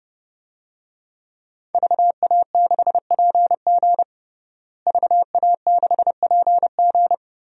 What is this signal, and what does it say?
Morse code, heard as a keyed tone, about 30 wpm, 700 Hz. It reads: VA6PG VA6PG
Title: DXCC-100 Call Signs - 5 Characters - Repeated 2x - 30wpm